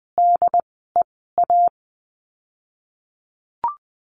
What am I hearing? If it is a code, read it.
Morse code, heard as a keyed tone, about 20 wpm, spaced as for 15 wpm, 700 Hz. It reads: DEA
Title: 108 Lesson 18f - Letter m - Words 20wpm